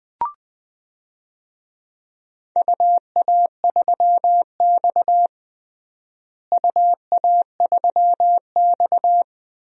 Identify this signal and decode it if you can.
Morse code, heard as a keyed tone, about 21 wpm, 700 Hz. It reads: UA3X UA3X
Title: DXCC-100 Call Signs - 4 Characters - Repeated 2x - 20wpm